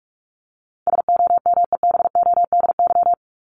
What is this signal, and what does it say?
Morse code, heard as a keyed tone, about 44 wpm, 700 Hz. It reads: SOMEBODY
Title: Top 1751-2000 Words 45wpm